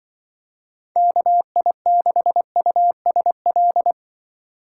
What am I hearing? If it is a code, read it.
Morse code, heard as a keyed tone, about 23 wpm, 700 Hz. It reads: KI6USL
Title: Speed Racing - USA 2x3 Call Signs 20wpm